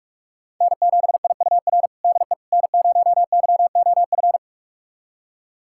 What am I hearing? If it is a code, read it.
Morse code, heard as a keyed tone, about 46 wpm, 700 Hz. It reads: N7IUR DE N0YOF
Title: USA Callsign DE Callsign - 5 Characters - 45wpm